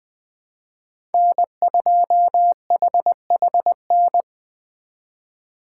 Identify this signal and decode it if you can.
Morse code, heard as a keyed tone, about 24 wpm, 700 Hz. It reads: N2HHN
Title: Speed Racing - USA 1x3 Call Signs 20wpm